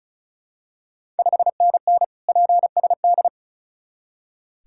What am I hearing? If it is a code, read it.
Morse code, heard as a keyed tone, about 36 wpm, 690 Hz. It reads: 5NN PSD
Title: Ontario QSO Party Exchange 35wpm